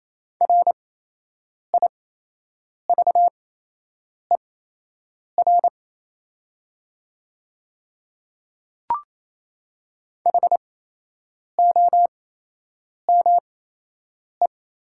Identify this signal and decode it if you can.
Morse code, heard as a keyed tone, about 28 wpm, spaced as for 8 wpm, 700 Hz. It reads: RIVER HOME
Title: 8x Character Spacing - Top 300 Words - 28wpm